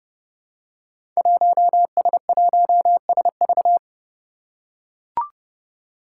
Morse code, heard as a keyed tone, about 30 wpm, 700 Hz. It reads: 1S1SV
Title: Five Challenging Letter-Number Combinations 30wpm